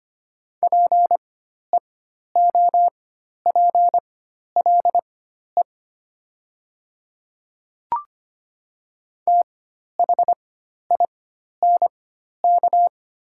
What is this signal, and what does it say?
Morse code, heard as a keyed tone, about 25 wpm, spaced as for 12 wpm, 700 Hz. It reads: PEOPLE THINK
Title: 4x Character Spacing - Top 200 Words - 25wpm